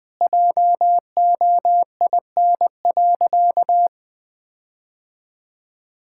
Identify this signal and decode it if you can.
Morse code, heard as a keyed tone, about 20 wpm, 700 Hz. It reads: JOIN.
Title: Sentences with 5-Word Phrases - 20wpm